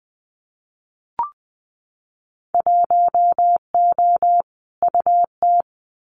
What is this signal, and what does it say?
Morse code, heard as a keyed tone, about 19 wpm, 700 Hz. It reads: 1O UT
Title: Winter Field Day Exchanges Encore 20wpm